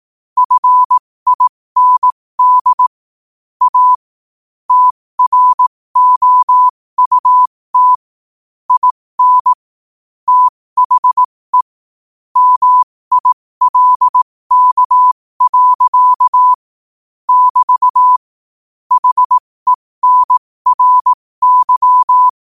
Morse code, heard as a keyed tone, about 18 wpm, spaced as for 15 wpm, 1000 Hz. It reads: FIND A TROUT IN THE MILK. <BT> HENRY
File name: qotdfiles / 15wpm